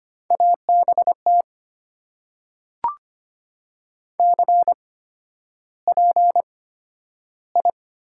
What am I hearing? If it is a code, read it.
Morse code, heard as a keyed tone, about 25 wpm, spaced as for 11 wpm, 700 Hz. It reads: ABT CPI
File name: Top 90 QSO Elements - 8x Character Spacing - 25wpm